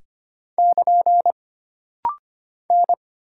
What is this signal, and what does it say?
Morse code, heard as a keyed tone, about 25 wpm, 700 Hz. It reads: <KN> N
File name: Single Character (Letter:Number:Punctuation:Prosign) - Rapid-Fire - 25wpm